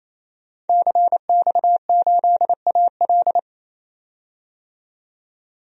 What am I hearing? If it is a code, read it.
Morse code, heard as a keyed tone, about 28 wpm, 700 Hz. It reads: CX8AL
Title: DXCC-100 Call Signs - 5 Characters - 28wpm